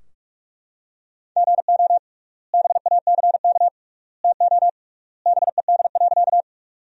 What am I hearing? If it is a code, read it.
Morse code, heard as a keyed tone, about 42 wpm, 700 Hz. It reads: GO BACK TO BED.
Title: 3x Word-Spacing - Sentences from Top 500 Words - 45wpm